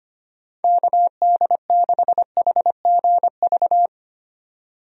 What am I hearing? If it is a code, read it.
Morse code, heard as a keyed tone, about 25 wpm, 700 Hz. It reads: KD6HGV
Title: USA Call Signs - 6 Characters - 25wpm